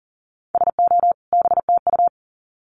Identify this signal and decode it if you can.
Morse code, heard as a keyed tone, about 40 wpm, 700 Hz. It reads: SO BTU
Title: Common QSO Phrases 40wpm